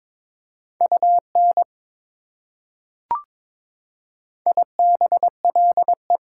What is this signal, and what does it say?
Morse code, heard as a keyed tone, about 21 wpm, 700 Hz. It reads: UN IBLE
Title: Prefixes and Suffixes 22wpm